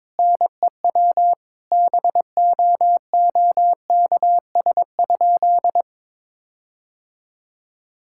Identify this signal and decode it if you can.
Morse code, heard as a keyed tone, about 22 wpm, 700 Hz. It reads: NEW BOOKS?
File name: Sentences with 9-Word Phrases - 22wpm